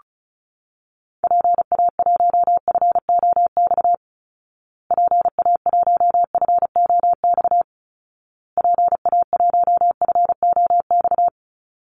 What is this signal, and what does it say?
Morse code, heard as a keyed tone, about 34 wpm, 700 Hz. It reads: PA1FOX PA1FOX PA1FOX
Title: Call Signs Repeated 2x - 35wpm